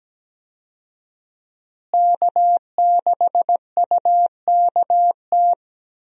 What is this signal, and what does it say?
Morse code, heard as a keyed tone, about 17 wpm, 700 Hz. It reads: K6UKT
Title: USA Call Signs - 5 Characters - Repeated 3x - 17wpm